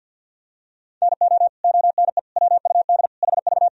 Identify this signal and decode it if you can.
Morse code, heard as a keyed tone, about 52 wpm, 700 Hz. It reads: NO ONE WUD HV
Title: Sentences from Top 300 Words and Abbreviations 50wpm